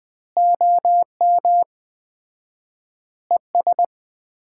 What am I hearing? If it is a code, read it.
Morse code, heard as a keyed tone, about 21 wpm, 700 Hz. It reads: OM ES
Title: Common QSO Phrases - 4x Word-Spacing - 20wpm